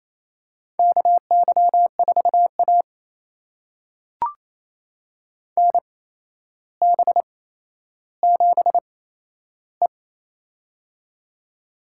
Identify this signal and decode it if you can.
Morse code, heard as a keyed tone, about 28 wpm, spaced as for 12 wpm, 700 Hz. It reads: KY4A NB7E
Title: USA Call Signs - 4 Characters - 8x Character Spacing - 28wpm